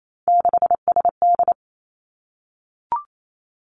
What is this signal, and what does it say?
Morse code, heard as a keyed tone, about 28 wpm, 700 Hz. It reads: 6SD
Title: Three Challenging Letter-Number Combinations 28wpm